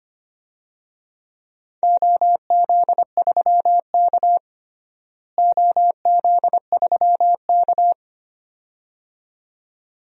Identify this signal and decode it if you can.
Morse code, heard as a keyed tone, about 25 wpm, 700 Hz. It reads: OZ3K OZ3K
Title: DXCC-100 Call Signs - 4 Characters - Repeated 2x - 25wpm